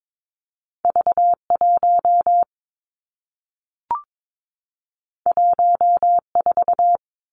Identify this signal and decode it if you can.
Morse code, heard as a keyed tone, about 22 wpm, 700 Hz. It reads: V1 14
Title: Two Challenging Letter-Number Combinations 22wpm